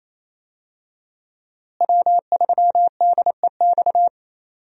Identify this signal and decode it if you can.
Morse code, heard as a keyed tone, about 28 wpm, 700 Hz. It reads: W3DEX
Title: USA Call Signs - 5 Characters - Repeated 2x - 28wpm